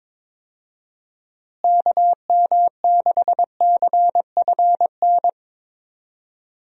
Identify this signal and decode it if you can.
Morse code, heard as a keyed tone, about 22 wpm, 700 Hz. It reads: KM6CFN
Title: USA Call Signs - 6 Characters - Repeated 2x - 22wpm